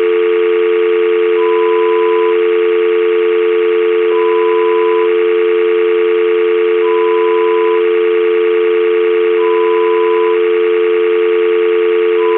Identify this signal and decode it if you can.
Dial tone (North America), 350 + 440 Hz, with noise.
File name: Dial-tone-sound-1--Lo-Fi-Preview-